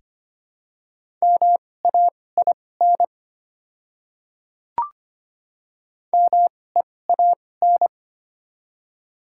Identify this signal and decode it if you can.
Morse code, heard as a keyed tone, about 25 wpm, spaced as for 18 wpm, 700 Hz. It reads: MAIN MEAN
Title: 2x Character Spacing - Top 300 Words - 25wpm